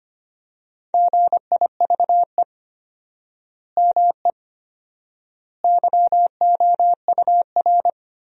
Morse code, heard as a keyed tone, about 25 wpm, 700 Hz. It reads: GIVE ME YOUR
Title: 4x Word-Spacing - Sentences from Top 200 Words - 25wpm